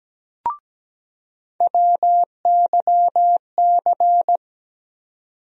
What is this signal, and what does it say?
Morse code, heard as a keyed tone, about 17 wpm, 700 Hz. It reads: WYC
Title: USA Call Sign Postfixes - 3 Letters 17wpm